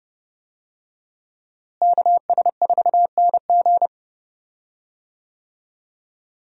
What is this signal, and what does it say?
Morse code, heard as a keyed tone, about 30 wpm, 700 Hz. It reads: KS4NG
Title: USA Call Signs - 5 Characters - Repeated 2x - 30wpm